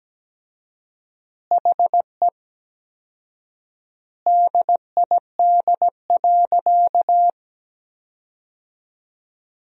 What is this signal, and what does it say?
Morse code, heard as a keyed tone, about 17 wpm, 700 Hz. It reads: HE DID.
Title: 4x Word-Spacing - Sentences from Top 500 Words - 17wpm